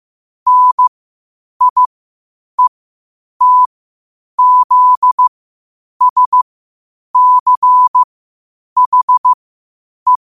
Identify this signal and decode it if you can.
Morse code, heard as a keyed tone, about 15 wpm, spaced as for 8 wpm, 1000 Hz. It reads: NIETZSCHE
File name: qotdfiles / 8wpm